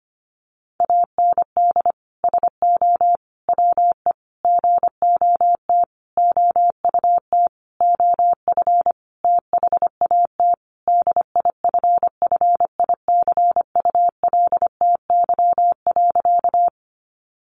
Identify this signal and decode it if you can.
Morse code, heard as a keyed tone, about 25 wpm, 700 Hz. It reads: AND SO WE GOT OUT OF THAT DIFFICULTY.
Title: Sentences from Top 1500 Words 25wpm